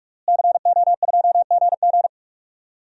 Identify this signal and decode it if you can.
Morse code, heard as a keyed tone, about 46 wpm, 700 Hz. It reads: CO2GG
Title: DXCC-100 Call Signs - 5 Characters - Repeated 3x - 45wpm